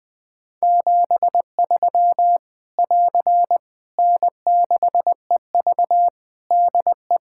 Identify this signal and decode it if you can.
Morse code, heard as a keyed tone, about 20 wpm, 700 Hz. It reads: 73 <AR> N6EV DE
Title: K1USN SST QSO Practice 20wpm